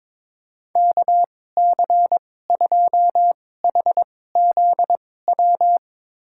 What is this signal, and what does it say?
Morse code, heard as a keyed tone, about 21 wpm, spaced as for 16 wpm, 700 Hz. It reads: KC2HZW
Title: USA Call Signs - 2x Character Spacing - 22wpm